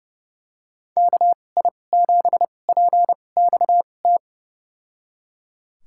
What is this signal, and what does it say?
Morse code, heard as a keyed tone, about 32 wpm, spaced as for 22 wpm, 700 Hz. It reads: KI7PXT
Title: USA Call Signs - 2x Character Spacing - 30wpm